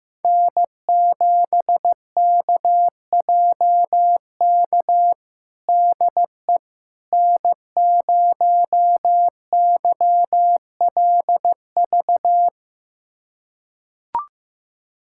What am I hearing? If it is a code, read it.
Morse code, heard as a keyed tone, about 15 wpm, 700 Hz. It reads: N7KJK DE N0YLV
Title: USA Callsign DE Callsign - 5 Characters - 15wpm